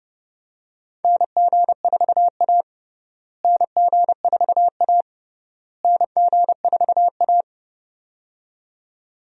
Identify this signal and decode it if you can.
Morse code, heard as a keyed tone, about 32 wpm, 700 Hz. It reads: NG4A NG4A NG4A
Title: USA Call Signs - 4 Characters - Repeated 3x - 30wpm